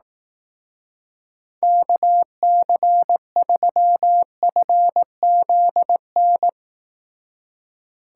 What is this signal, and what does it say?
Morse code, heard as a keyed tone, about 18 wpm, 700 Hz. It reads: KC3FZN
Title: Speed Racing - USA 2x3 Call Signs 15wpm